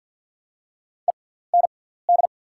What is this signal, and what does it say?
Morse code, heard as a keyed tone, about 48 wpm, spaced as for 17 wpm, 700 Hz. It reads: END
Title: 6x Character Spacing - Top 200 Words - 50wpm